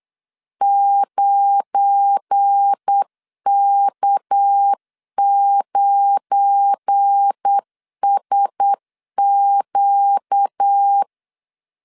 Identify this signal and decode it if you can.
Morse code, heard as a keyed tone, about 8 wpm, 800 Hz. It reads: 9K9SQ
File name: LEZIONI / 7 GF9Z / 7 GF9 8wpm 3545